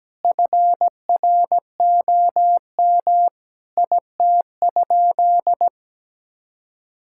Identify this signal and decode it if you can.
Morse code, heard as a keyed tone, about 17 wpm, 700 Hz. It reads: FROM IT?
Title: Sentences from Top 1000 Words - Review Part 5 - 17wpm